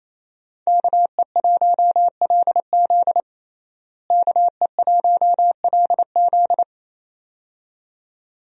Morse code, heard as a keyed tone, about 28 wpm, 700 Hz. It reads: KE1LZ KE1LZ
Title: USA Call Signs - 5 Characters - Repeated 2x - 28wpm